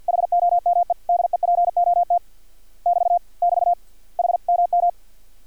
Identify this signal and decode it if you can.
Morse code, heard as a keyed tone, about 50 wpm, 700 Hz. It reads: HOME DEPOT <BT> <BT> HMM
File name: archived / CRSnet